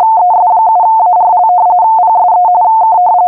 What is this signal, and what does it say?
RTTY, about 45.45 baud. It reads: ES7ARL PSE K